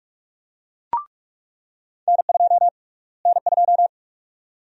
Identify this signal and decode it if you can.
Morse code, heard as a keyed tone, about 44 wpm, 700 Hz. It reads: N2 N2
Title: USA Call Sign Prefixes - Repeated 2x - 45wpm